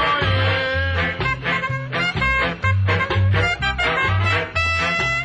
Emergency Alert System (SAME) tones, heard as harsh buzzing bursts. It is heard from 0.6 to 4.0 s.